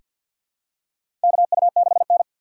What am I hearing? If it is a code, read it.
Morse code, heard as a keyed tone, about 47 wpm, 700 Hz. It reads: KU6N
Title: USA Call Signs - 4 Characters - 3x Character Spacing - 50wpm